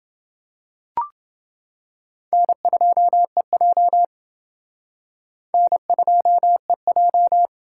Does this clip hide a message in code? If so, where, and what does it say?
Yes: Morse code from 2.3 to 7.5 s, heard as a keyed tone, about 28 wpm, 700 Hz. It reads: N2EJ N2EJ